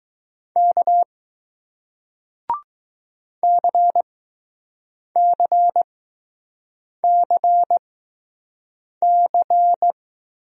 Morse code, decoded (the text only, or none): K CCCC